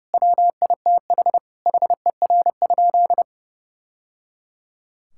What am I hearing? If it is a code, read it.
Morse code, heard as a keyed tone, about 31 wpm, 700 Hz. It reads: WITH HER?
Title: Sentences from Top 1000 Words - Review Part 5 - 30wpm